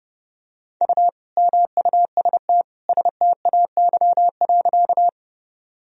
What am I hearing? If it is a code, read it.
Morse code, heard as a keyed tone, about 30 wpm, 700 Hz. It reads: U MUST STAY.
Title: Sentences from Top 500 Words and Abbreviations 30wpm